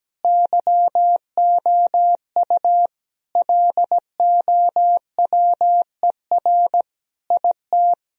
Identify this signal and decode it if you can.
Morse code, heard as a keyed tone, about 17 wpm, 700 Hz. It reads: YOU LOWER IT
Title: Sentences from Top 1000 Words - Review Part 4 - 17wpm